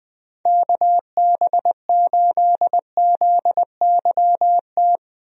Morse code, heard as a keyed tone, about 20 wpm, 700 Hz. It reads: KB8ZYT
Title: USA Call Signs - 6 Characters - Repeated 3x - 20wpm